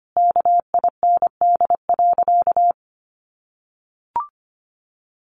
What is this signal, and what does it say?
Morse code, heard as a keyed tone, about 25 wpm, 700 Hz. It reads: KIND.
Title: Sentences from Top 1500 Words 25wpm